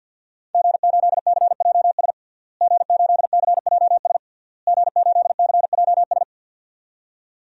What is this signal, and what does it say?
Morse code, heard as a keyed tone, about 47 wpm, 700 Hz. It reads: G8CJS G8CJS G8CJS
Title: DXCC-100 Call Signs - 5 Characters - Repeated 3x - 50wpm